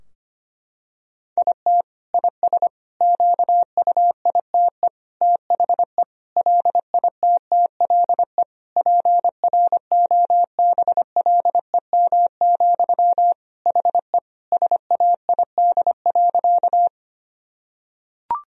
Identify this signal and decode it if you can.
Morse code, heard as a keyed tone, about 25 wpm, 700 Hz. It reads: IT IS QUITE THE LITTLE PROBLEM, HE SAID.
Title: Sentences from Top 1000 Words - Review Part 2 - 25wpm